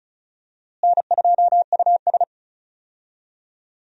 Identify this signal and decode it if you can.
Morse code, heard as a keyed tone, about 34 wpm, 700 Hz. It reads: N2US
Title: USA Call Signs - 4 Characters - Repeated 3x - 35wpm